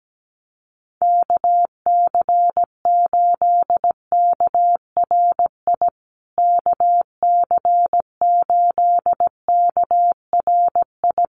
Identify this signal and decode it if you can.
Morse code, heard as a keyed tone, about 17 wpm, 700 Hz. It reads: KC8KRI KC8KRI
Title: USA Call Signs Repeated 3x 17wpm